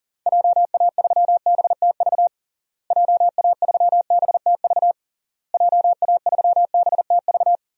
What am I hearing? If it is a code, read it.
Morse code, heard as a keyed tone, about 40 wpm, 700 Hz. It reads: JA3BTV JA3BTV JA3BTV
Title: DXCC-100 Call Signs - 6 Characters - Repeated 3x - 40wpm